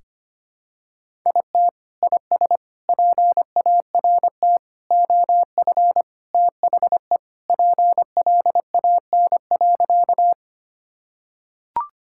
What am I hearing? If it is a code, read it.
Morse code, heard as a keyed tone, about 25 wpm, 700 Hz. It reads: IT IS PART OF THE PLAN.
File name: practice / 25wpm / Sentences with 4-Word Phrases - 25wpm